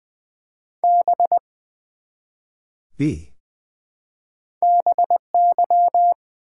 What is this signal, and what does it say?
Morse code, heard as a keyed tone, about 20 wpm, 700 Hz. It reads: B BY
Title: Build A Word - 1 Letter At A Time - 3-Letter Words - 20wpm